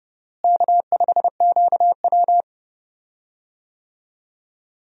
Morse code, heard as a keyed tone, about 30 wpm, 700 Hz. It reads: K5QW
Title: USA Call Signs - 4 Characters - Repeated 3x - 30wpm